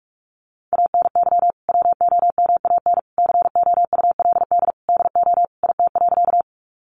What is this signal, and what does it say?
Morse code, heard as a keyed tone, about 44 wpm, 700 Hz. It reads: ANY WOMAN COULD DO IT.
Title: Sentences from Top 1100 Words 45wpm